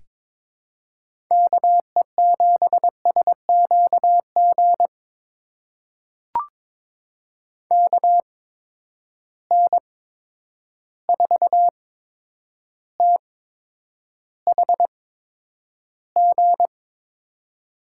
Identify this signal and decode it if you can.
Morse code, heard as a keyed tone, about 22 wpm, spaced as for 9 wpm, 700 Hz. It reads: KE7SQG KN4THG
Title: USA Call Signs - 6 Characters - 8x Character Spacing - 22wpm